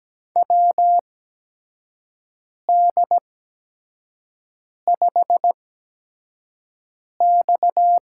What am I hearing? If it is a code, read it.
Morse code, heard as a keyed tone, about 17 wpm, spaced as for 5 wpm, 700 Hz. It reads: WD5X
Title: USA Call Signs - 4 Characters - 8x Character Spacing - 17wpm